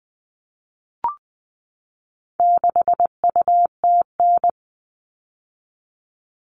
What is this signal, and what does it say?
Morse code, heard as a keyed tone, about 20 wpm, 700 Hz. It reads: 6UTN
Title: All Cut Numbers - 4-Digit 20wpm